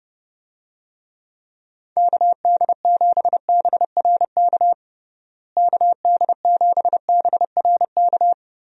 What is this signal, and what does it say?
Morse code, heard as a keyed tone, about 32 wpm, 700 Hz. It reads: KD7BRK KD7BRK
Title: USA Call Signs - 6 Characters - Repeated 3x - 30wpm